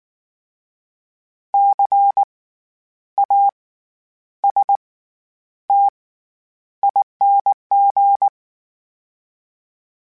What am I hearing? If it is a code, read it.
Morse code, heard as a keyed tone, about 19 wpm, spaced as for 9 wpm, 800 Hz. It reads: CASTING
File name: Special INGs / Special INGs-mp3s / SlowToFaster ing @18